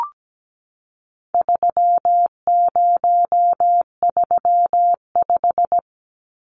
Morse code, heard as a keyed tone, about 17 wpm, 700 Hz. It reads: 3035
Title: Numbers - 4-Digit 17wpm